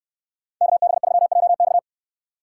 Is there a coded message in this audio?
Yes: Morse code from 0.6 to 1.8 s, heard as a keyed tone, about 64 wpm, 700 Hz. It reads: KB3JX